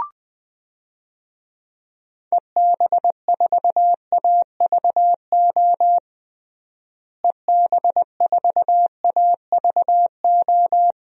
Morse code, heard as a keyed tone, about 20 wpm, 700 Hz. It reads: EB4AVO EB4AVO
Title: DXCC-100 Call Signs - 6 Characters - Repeated 3x - 20wpm